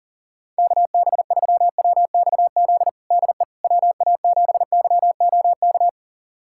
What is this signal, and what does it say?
Morse code, heard as a keyed tone, about 37 wpm, 700 Hz. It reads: KB3WXZ DE WA7YOK
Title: USA Callsign DE Callsign - 6 Characters - 40wpm